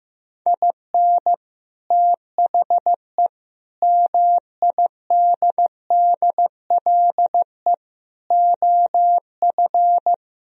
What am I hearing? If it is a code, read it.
Morse code, heard as a keyed tone, about 15 wpm, 700 Hz. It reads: IN THE MIDDLE OF
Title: Speed-Racing - Sets of 5 Words 15wpm